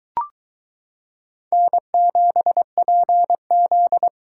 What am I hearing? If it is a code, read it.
Morse code, heard as a keyed tone, about 23 wpm, 700 Hz. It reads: N7PZ